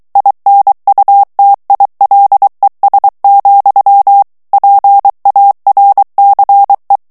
Morse code, heard as a keyed tone, about 23 wpm, 800 Hz. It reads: INUTILES, PARCE
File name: FRANCAIS TEXTE CLAIR 2 VITESSE 22 MOTS MINUTES